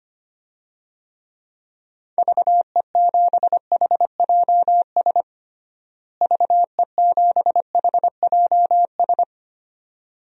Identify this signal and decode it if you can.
Morse code, heard as a keyed tone, about 25 wpm, 700 Hz. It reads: VE7HJS VE7HJS
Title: DXCC-100 Call Signs - 6 Characters - Repeated 2x - 25wpm